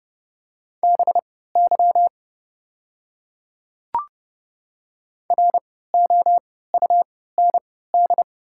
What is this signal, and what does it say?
Morse code, heard as a keyed tone, about 30 wpm, spaced as for 17 wpm, 700 Hz. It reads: BY ROUND
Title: 3x Character Spacing - Top 500 Words - 30wpm